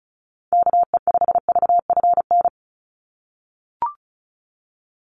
Morse code, heard as a keyed tone, about 34 wpm, 700 Hz. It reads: KE5VFN
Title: USA Call Signs 35wpm